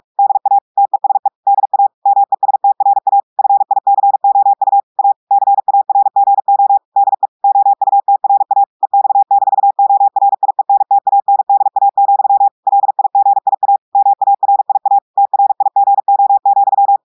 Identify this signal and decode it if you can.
Morse code, heard as a keyed tone, about 46 wpm, 800 Hz. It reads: DA TESE DA MESTRA FICOU A CARGO DE OUTRA EX-ORIENTANDA, LIGIA MARIA TRIGO,
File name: audios / portugais / portugais texte en clair 1 vitesse 45 mots minutes